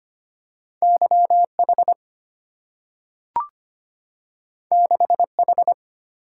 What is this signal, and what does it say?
Morse code, heard as a keyed tone, about 25 wpm, 700 Hz. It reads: YH 6H
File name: Two Letter-Number Combinations - 25wpm